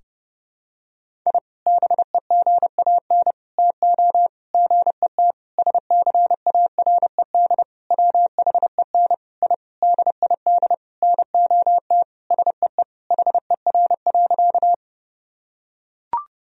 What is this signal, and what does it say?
Morse code, heard as a keyed tone, about 30 wpm, 700 Hz. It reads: I BEGAN TO GET SCARED WHEN I DID NOT SEE HER.
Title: Sentences from Top 1000 Words - Review Part 4 - 30wpm